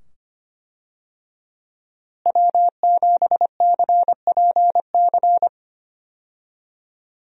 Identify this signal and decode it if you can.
Morse code, heard as a keyed tone, about 25 wpm, 700 Hz. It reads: W7CPC
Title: USA Call Signs - 5 Characters - Repeated 2x - 25wpm